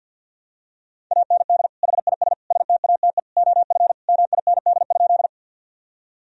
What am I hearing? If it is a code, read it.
Morse code, heard as a keyed tone, about 49 wpm, 700 Hz. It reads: AND HIS STATE OF MIND?
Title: Sentences from Top 1000 Words - Review Part 2 - 50wpm